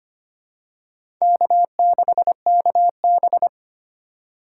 Morse code, heard as a keyed tone, about 25 wpm, 700 Hz. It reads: K6KB
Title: USA Call Signs - 4 Characters - 2x Character Spacing - 25wpm